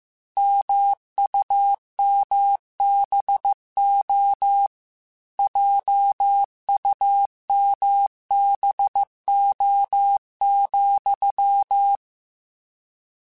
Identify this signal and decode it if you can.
Morse code, heard as a keyed tone, about 15 wpm, 800 Hz. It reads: MUMBO JUMBO,
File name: Misc / Misc-mp3s / Word Pairs @13